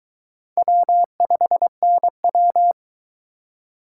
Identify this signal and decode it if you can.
Morse code, heard as a keyed tone, about 23 wpm, 700 Hz. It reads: W5NW